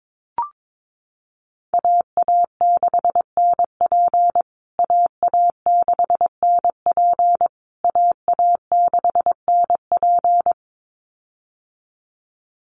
Morse code, heard as a keyed tone, about 22 wpm, 700 Hz. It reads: AA6NP AA6NP AA6NP
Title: USA Call Signs Repeated 3x 22wpm